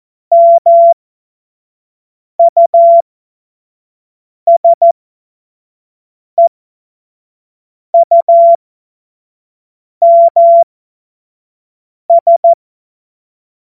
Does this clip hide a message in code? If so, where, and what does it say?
Yes: Morse code from 0.3 to 12.5 s, heard as a keyed tone, about 14 wpm, spaced as for 5 wpm, 680 Hz. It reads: MUSEUMS